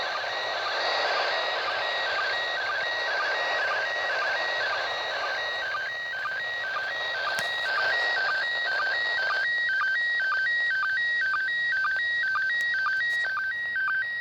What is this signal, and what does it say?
SSTV (slow-scan television), PD 120.
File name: ARISS-SSTV-Expedition-63-ARISS-Series-16-Satellites